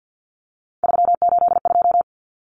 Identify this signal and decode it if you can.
Morse code, heard as a keyed tone, about 50 wpm, 700 Hz. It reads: 382
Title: Common Cut Numbers - 3-Digit 50wpm